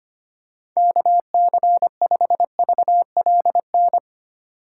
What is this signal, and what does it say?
Morse code, heard as a keyed tone, about 25 wpm, 690 Hz. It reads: KC5VLN
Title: USA Call Signs - 6 Characters - 3x Character Spacing - 25wpm